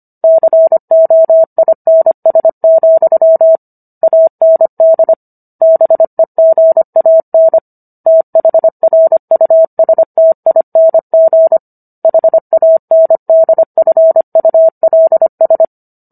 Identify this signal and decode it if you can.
Morse code, heard as a keyed tone, about 25 wpm, 650 Hz. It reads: COINS, AND BEGAN THRUSTING HANDFULS